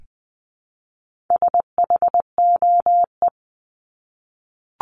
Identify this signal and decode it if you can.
Morse code, heard as a keyed tone, about 20 wpm, 700 Hz. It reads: SHOE
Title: 260 Lesson 42e - Words 20-20wpm